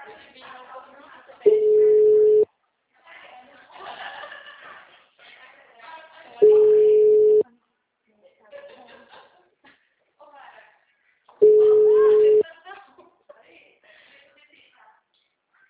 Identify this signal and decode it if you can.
Ringback tone (Europe, 425 Hz), with speech.